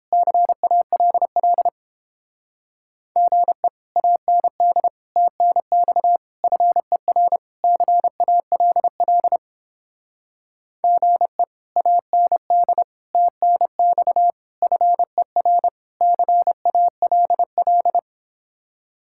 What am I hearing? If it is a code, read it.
Morse code, heard as a keyed tone, about 29 wpm, 700 Hz. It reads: CALL GE AND TNX FER CALL GE AND TNX FER CALL